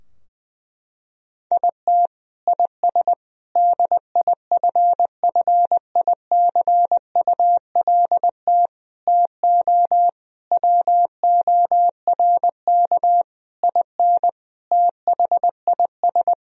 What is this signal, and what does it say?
Morse code, heard as a keyed tone, about 20 wpm, 700 Hz. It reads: IT IS DIFFICULT TO WORK IN THIS
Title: Sentences from Top 1000 Words - Review Part 3 - 20wpm